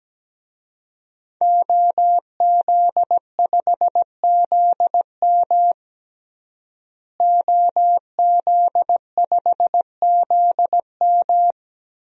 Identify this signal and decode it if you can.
Morse code, heard as a keyed tone, about 17 wpm, 700 Hz. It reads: OZ5ZM OZ5ZM
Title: DXCC-100 Call Signs - 5 Characters - Repeated 3x - 17wpm